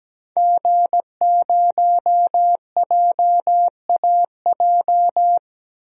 Morse code, heard as a keyed tone, about 17 wpm, 700 Hz. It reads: G0JAJ
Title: DXCC-100 Call Signs - 5 Characters - 17wpm